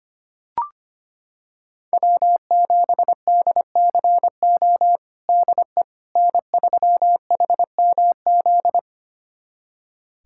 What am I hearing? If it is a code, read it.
Morse code, heard as a keyed tone, about 25 wpm, 700 Hz. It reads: W7DCO DE N3HMZ
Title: USA Callsign DE Callsign - 5 Characters - 25wpm